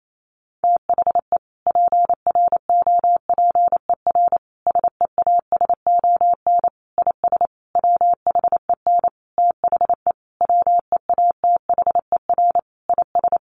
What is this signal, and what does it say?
Morse code, heard as a keyed tone, about 28 wpm, 700 Hz. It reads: THE PROPER SEASON IS WHEN THE WEATHER IS
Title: Sentences from Top 1400 Words 28wpm